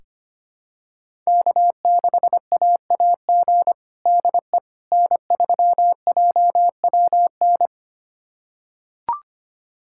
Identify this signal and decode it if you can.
Morse code, heard as a keyed tone, about 25 wpm, 700 Hz. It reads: K6AAG DE N3JWN
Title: USA Callsign DE Callsign - 5 Characters - 25wpm